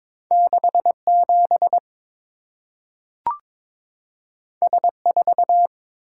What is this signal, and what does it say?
Morse code, heard as a keyed tone, about 22 wpm, 700 Hz. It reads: 67 S4
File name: practice / 22wpm / Two Letter-Number Combinations - 22wpm